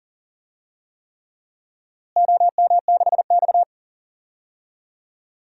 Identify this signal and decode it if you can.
Morse code, heard as a keyed tone, about 40 wpm, 700 Hz. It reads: OM6X
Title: DXCC-100 Call Signs - 4 Characters - Repeated 3x - 40wpm